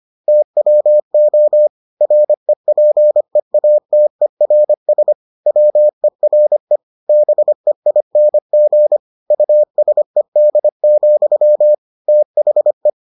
Morse code, heard as a keyed tone, about 25 wpm, 600 Hz. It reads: TWO REPEATERS WERE BEING USED, THE